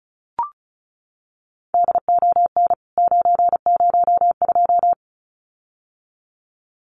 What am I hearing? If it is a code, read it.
Morse code, heard as a keyed tone, about 34 wpm, 700 Hz. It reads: DON 902
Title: CWT Exchange 35wpm